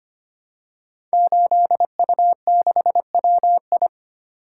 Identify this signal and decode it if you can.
Morse code, heard as a keyed tone, about 25 wpm, 700 Hz. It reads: 8U6WI
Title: Code Groups - 5 Letter:Numbers - 25wpm